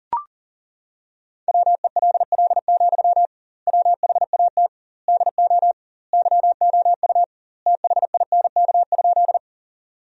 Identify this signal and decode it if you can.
Morse code, heard as a keyed tone, about 40 wpm, 700 Hz. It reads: WELL, WHAT DO YOU THINK?
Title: 2x Word-Spacing - Sentences from Top 200 Words - 40wpm